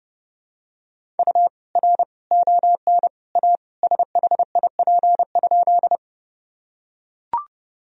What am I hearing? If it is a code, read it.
Morse code, heard as a keyed tone, about 30 wpm, 700 Hz. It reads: U R ON A SHIP?
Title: Sentences from Top 400 Words and Abbreviations 30wpm